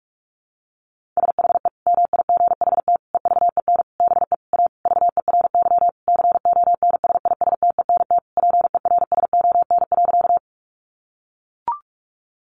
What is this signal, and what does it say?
Morse code, heard as a keyed tone, about 44 wpm, 700 Hz. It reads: SHE MIGHT EVEN BE A VERY CONSISTENT PERSON.
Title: Sentences from Top 1500 Words 45wpm